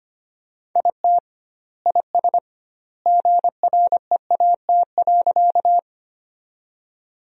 Morse code, heard as a keyed tone, about 25 wpm, 700 Hz. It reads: IT IS GREAT.
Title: 2x Word-Spacing - Sentences from Top 200 Words - 25wpm